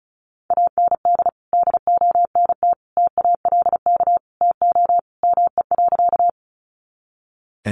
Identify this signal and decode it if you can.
Morse code, heard as a keyed tone, about 34 wpm, 700 Hz. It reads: AND DONT TALK TO ME.
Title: Sentences from Top 400 Words 35wpm